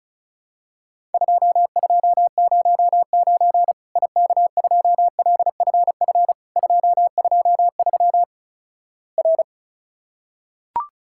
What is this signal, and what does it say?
Morse code, heard as a keyed tone, about 36 wpm, 700 Hz. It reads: 2209 IK2LFF 223 R
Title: Worked All Europe DX Contest - QTC Exchange - 35wpm